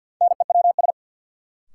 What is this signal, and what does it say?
Morse code, heard as a keyed tone, about 48 wpm, 700 Hz. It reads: NEWS
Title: Sets of 5 Words - New Word Drill - 50wpm